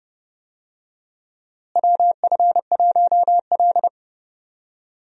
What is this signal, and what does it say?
Morse code, heard as a keyed tone, about 32 wpm, 700 Hz. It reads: WF1L
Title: USA Call Signs - 4 Characters - Repeated 3x - 30wpm